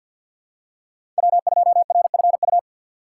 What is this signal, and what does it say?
Morse code, heard as a keyed tone, about 51 wpm, 690 Hz. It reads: W2RFU
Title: USA Call Signs - 5 Characters - 3x Character Spacing - 50wpm